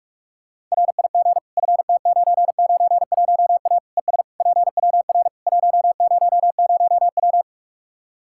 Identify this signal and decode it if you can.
Morse code, heard as a keyed tone, about 45 wpm, 700 Hz. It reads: RIG FT991A ES PWR 100W
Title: Practice QSO - Exchange 2 - Info - 45wpm